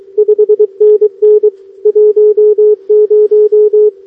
Morse code, heard as a keyed tone, about 22 wpm, 420 Hz. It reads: 5NN 10
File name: OK QRP 2026 / OKQRP2026